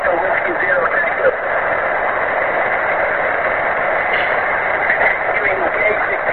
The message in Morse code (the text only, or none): none